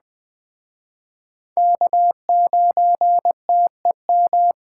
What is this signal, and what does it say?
Morse code, heard as a keyed tone, about 21 wpm, 700 Hz. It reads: K9TEM
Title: Speed Racing - USA 1x3 Call Signs 17wpm